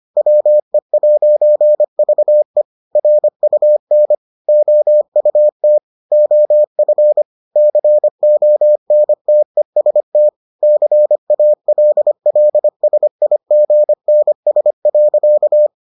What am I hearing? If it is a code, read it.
Morse code, heard as a keyed tone, about 25 wpm, 600 Hz. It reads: WE'VE RUN OUT OF CONTEST CALLSIGNS.